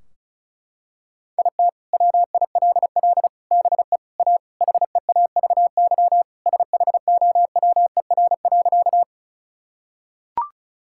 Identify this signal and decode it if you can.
Morse code, heard as a keyed tone, about 34 wpm, 700 Hz. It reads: IT WILL BE A HEAVY SHOWER.
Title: Sentences from Top 1600 Words 35wpm